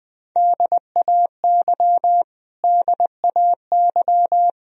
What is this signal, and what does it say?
Morse code, heard as a keyed tone, about 20 wpm, 700 Hz. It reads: DAY DAY
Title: Top 100 Words Repeated 20wpm